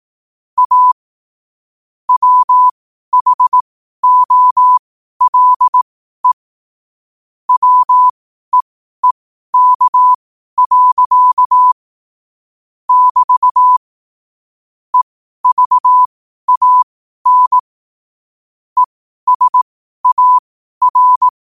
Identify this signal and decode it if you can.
Morse code, heard as a keyed tone, about 18 wpm, spaced as for 12 wpm, 1000 Hz. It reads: A WHOLE WEEK. <BT> EVAN ESAR
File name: qotdfiles / 12wpm